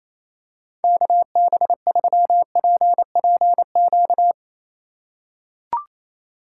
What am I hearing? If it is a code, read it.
Morse code, heard as a keyed tone, about 28 wpm, 700 Hz. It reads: KB3PPQ
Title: USA Call Signs - 2x Character Spacing - 28wpm